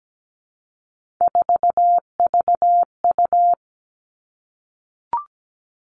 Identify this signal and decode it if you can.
Morse code, heard as a keyed tone, about 17 wpm, 700 Hz. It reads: 4VU
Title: Three Challenging Letter-Number Combinations 17wpm